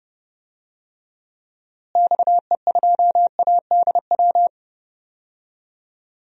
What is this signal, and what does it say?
Morse code, heard as a keyed tone, about 30 wpm, 700 Hz. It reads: XE2ADW
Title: DXCC-100 Call Signs - 6 Characters - Repeated 3x - 30wpm